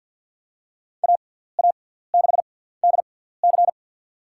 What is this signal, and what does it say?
Morse code, heard as a keyed tone, about 47 wpm, spaced as for 17 wpm, 700 Hz. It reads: AA6DC
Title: USA Call Signs - 6x Character Spacing - 50wpm